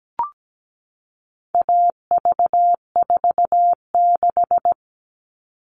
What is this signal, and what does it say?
Morse code, heard as a keyed tone, about 17 wpm, 700 Hz. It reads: AV46
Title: All Cut Numbers - 4-Digit 17wpm